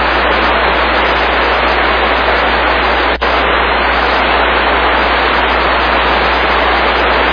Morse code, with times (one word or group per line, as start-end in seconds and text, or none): none